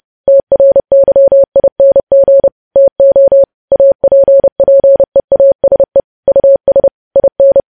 Morse code, heard as a keyed tone, about 30 wpm, 550 Hz. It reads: TRYING TO APPEASE US IN
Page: News Headlines in Morse Code at 30 WPM